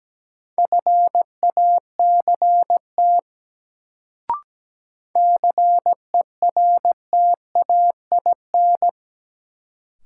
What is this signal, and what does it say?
Morse code, heard as a keyed tone, about 17 wpm, 700 Hz. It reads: FACT CERTAIN
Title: Sentences from Top 500 Words - New Word Drill - 17wpm